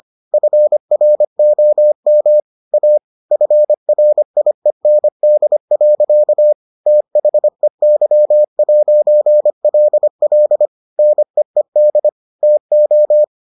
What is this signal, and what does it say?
Morse code, heard as a keyed tone, about 25 wpm, 600 Hz. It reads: FROM A FRIEND. THEY'LL NEED TO